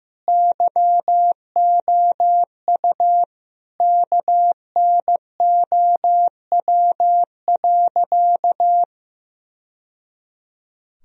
Morse code, heard as a keyed tone, about 15 wpm, 700 Hz. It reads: YOU KNOW.
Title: Sentences from Top 1000 Words - Review Part 3 - 15wpm